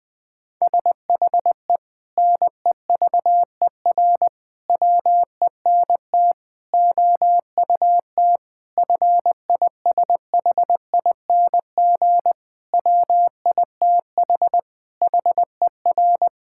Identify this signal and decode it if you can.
Morse code, heard as a keyed tone, about 21 wpm, 700 Hz. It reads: SHE NEVER WENT OUT FISHING WITH HER
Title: Sentences from Top 1000 Words - Review Part 5 - 20wpm